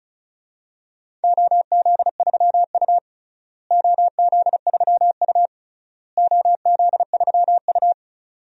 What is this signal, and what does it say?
Morse code, heard as a keyed tone, about 34 wpm, 700 Hz. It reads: OZ3U OZ3U OZ3U
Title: DXCC-100 Call Signs - 4 Characters - Repeated 3x - 35wpm